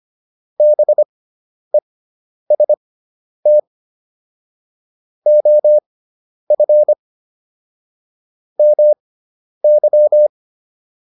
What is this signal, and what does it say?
Morse code, heard as a keyed tone, about 25 wpm, spaced as for 10 wpm, 600 Hz. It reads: BEST OF MY